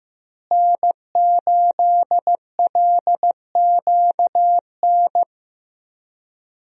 Morse code, heard as a keyed tone, about 15 wpm, 700 Hz. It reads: N8LQN
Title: USA Call Signs - 5 Characters - Repeated 2x - 15wpm